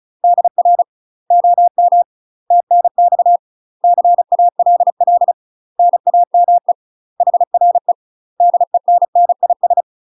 Morse code, heard as a keyed tone, about 34 wpm, 700 Hz. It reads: DR OM TNX CALL NAME HRE DENNIS
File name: UK MASTER QSOs TRAINING / 35WPM / UK MASTER QSOs 0000